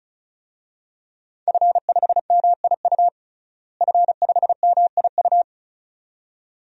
Morse code, heard as a keyed tone, about 34 wpm, 700 Hz. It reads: F5MIU F5MIU
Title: DXCC-100 Call Signs - 5 Characters - Repeated 2x - 35wpm